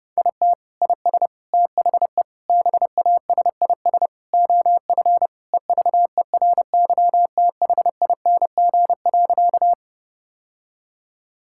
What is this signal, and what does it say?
Morse code, heard as a keyed tone, about 30 wpm, 700 Hz. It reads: IT IS THE BASIS OF EVERYTHING.
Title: Sentences from Top 1000 Words - Review Part 5 - 30wpm